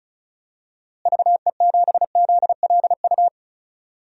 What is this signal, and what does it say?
Morse code, heard as a keyed tone, about 34 wpm, 700 Hz. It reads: VE7ZLU
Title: DXCC-100 Call Signs - 6 Characters - Repeated 3x - 35wpm